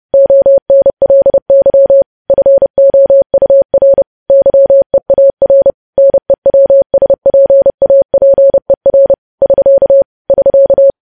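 Morse code, heard as a keyed tone, about 30 wpm, 550 Hz. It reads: ONLY FOUR YEAR NEWSPAPER <SK> <SK>
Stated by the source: Morse news headlines sent at 30 wpm.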